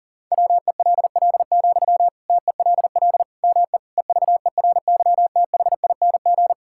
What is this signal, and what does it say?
Morse code, heard as a keyed tone, about 40 wpm, 700 Hz. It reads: WELL, TELL ME EVERYTHING
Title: Sentences from Top 1000 Words - Review Part 6 - 40wpm